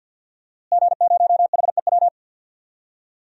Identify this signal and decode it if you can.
Morse code, heard as a keyed tone, about 52 wpm, 700 Hz. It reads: G0HEW
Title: DXCC-100 Call Signs - 5 Characters - Repeated 3x - 50wpm